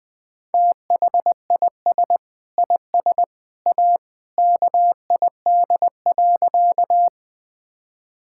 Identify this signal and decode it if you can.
Morse code, heard as a keyed tone, about 20 wpm, 700 Hz. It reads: THIS IS A KID.
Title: Sentences from Top 1600 Words 20wpm